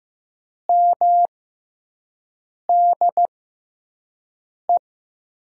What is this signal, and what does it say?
Morse code, heard as a keyed tone, about 15 wpm, spaced as for 5 wpm, 700 Hz. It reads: MDE
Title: USA Call Signs - 6 Characters - 6x Character Spacing - 15wpm